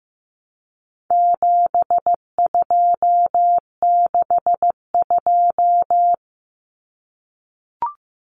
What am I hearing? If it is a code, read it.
Morse code, heard as a keyed tone, about 15 wpm, 700 Hz. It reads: 7262
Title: Common Cut Numbers - 4-Digit 15wpm